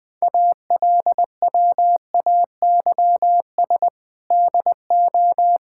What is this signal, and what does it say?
Morse code, heard as a keyed tone, about 20 wpm, 700 Hz. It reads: ALWAYS DO
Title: Sentences with 8-Word Phrases - Spoken Beginning - 20wpm